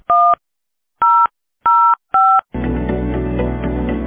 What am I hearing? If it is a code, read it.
DTMF, heard as telephone touch tones, tones of about 280 ms. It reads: 1005